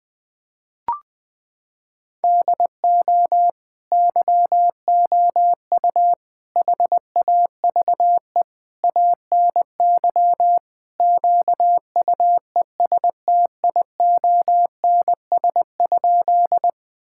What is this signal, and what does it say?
Morse code, heard as a keyed tone, about 21 wpm, 700 Hz. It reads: DO YOU HAVE ANY QUESTIONS?
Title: Sentences with 4-Word Phrases - 20wpm